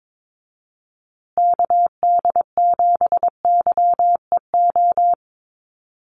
Morse code, heard as a keyed tone, about 22 wpm, 700 Hz. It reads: KD7YEO
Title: USA Call Signs Repeated 2x 22wpm